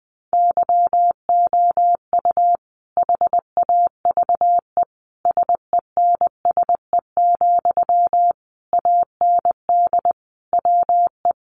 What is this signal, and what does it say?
Morse code, heard as a keyed tone, about 20 wpm, 700 Hz. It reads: YOU HAVE SENSE, AND WE
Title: Sentences from Top 1100 Words 20wpm